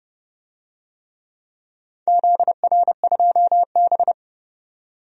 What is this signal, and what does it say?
Morse code, heard as a keyed tone, about 30 wpm, 700 Hz. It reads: ZR2B
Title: DXCC-100 Call Signs - 4 Characters - 30wpm